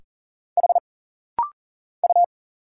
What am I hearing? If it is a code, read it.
Morse code, heard as a keyed tone, about 40 wpm, 700 Hz. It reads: H U
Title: Single Letter-Number - Rapid-Fire 40wpm